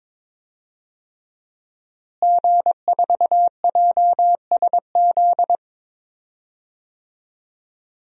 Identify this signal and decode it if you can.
Morse code, heard as a keyed tone, about 22 wpm, 700 Hz. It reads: G4JSZ
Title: DXCC-100 Call Signs - 5 Characters - Repeated 3x - 22wpm